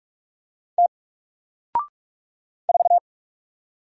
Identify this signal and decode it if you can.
Morse code, heard as a keyed tone, about 44 wpm, 700 Hz. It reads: T 4
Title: Single Letter-Number Alt 45wpm